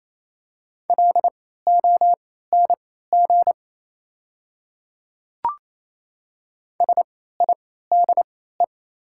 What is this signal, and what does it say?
Morse code, heard as a keyed tone, about 28 wpm, spaced as for 16 wpm, 700 Hz. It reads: LONG SIDE
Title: 3x Character Spacing - Top 100 Words - 28wpm